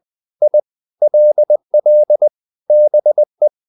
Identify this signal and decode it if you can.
Morse code, heard as a keyed tone, about 20 wpm, 600 Hz. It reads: I LL BE
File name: the old man and the sea part 2 20wpm